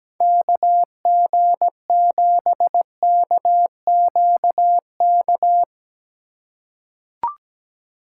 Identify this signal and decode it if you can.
Morse code, heard as a keyed tone, about 17 wpm, 700 Hz. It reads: KG7KQK
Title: USA Call Signs - 6 Characters - 4x Character Spacing - 17wpm